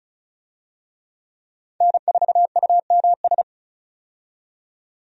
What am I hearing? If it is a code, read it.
Morse code, heard as a keyed tone, about 36 wpm, 700 Hz. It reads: N4UMS
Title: USA Call Signs - 5 Characters - 35wpm